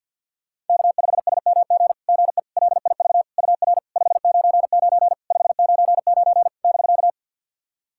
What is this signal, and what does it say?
Morse code, heard as a keyed tone, about 48 wpm, 690 Hz. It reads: K5SMG GE LIV UR 599 599 <BK>
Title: SOTA QSO Practice 50wpm